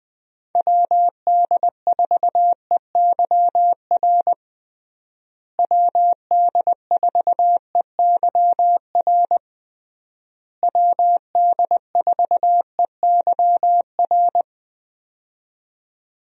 Morse code, heard as a keyed tone, about 20 wpm, 700 Hz. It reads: WD4EYR WD4EYR WD4EYR
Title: USA Call Signs - 6 Characters - Repeated 3x - 20wpm